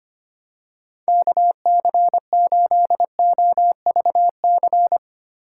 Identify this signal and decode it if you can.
Morse code, heard as a keyed tone, about 25 wpm, 700 Hz. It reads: KC8OVC
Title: USA Call Signs - 6 Characters - 3x Character Spacing - 25wpm